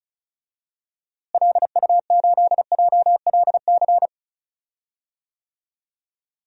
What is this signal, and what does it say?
Morse code, heard as a keyed tone, about 34 wpm, 700 Hz. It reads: LU8JLC
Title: DXCC-100 Call Signs - 6 Characters - Repeated 2x - 35wpm